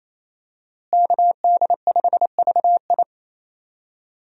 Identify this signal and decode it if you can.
Morse code, heard as a keyed tone, about 27 wpm, 700 Hz. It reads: KD5VI
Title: USA Call Signs - 5 Characters - Repeated 3x - 28wpm